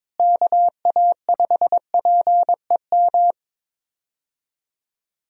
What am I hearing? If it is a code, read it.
Morse code, heard as a keyed tone, about 22 wpm, 700 Hz. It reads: KA5PEM
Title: USA Call Signs - 6 Characters - Repeated 2x - 22wpm